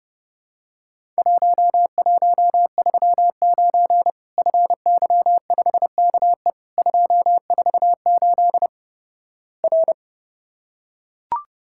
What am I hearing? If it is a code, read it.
Morse code, heard as a keyed tone, about 32 wpm, 700 Hz. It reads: 1139 FY5KE 248 R
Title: Worked All Europe DX Contest - QTC Exchange - 30wpm